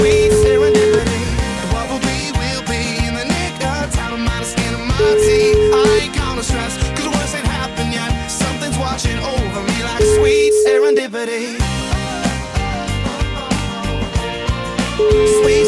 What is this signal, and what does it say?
Ringback tone (Europe, 425 Hz), with noise.